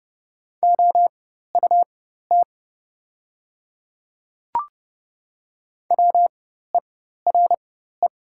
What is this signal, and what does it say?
Morse code, heard as a keyed tone, about 30 wpm, spaced as for 14 wpm, 690 Hz. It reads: OUT WERE
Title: 4x Character Spacing - Top 200 Words - 30wpm